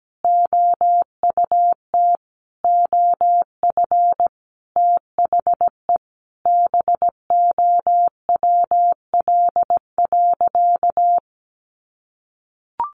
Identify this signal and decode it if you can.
Morse code, heard as a keyed tone, about 17 wpm, 700 Hz. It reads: OUT OF THE BOWL.
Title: Sentences from Top 1200 Words 17wpm